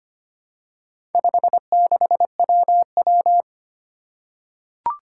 Morse code, heard as a keyed tone, about 25 wpm, 700 Hz. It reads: 56WW
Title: Four Challenging Letter-Number Combinations 25wpm